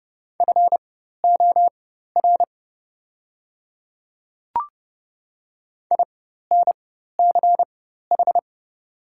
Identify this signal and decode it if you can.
Morse code, heard as a keyed tone, about 30 wpm, spaced as for 14 wpm, 700 Hz. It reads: FOR INCH
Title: 4x Character Spacing - Top 500 Words - 30wpm